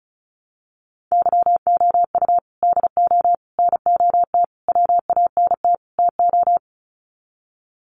Morse code, heard as a keyed tone, about 34 wpm, 700 Hz. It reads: YOU DO NOT WANT TO
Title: Sets of 5 Words 35wpm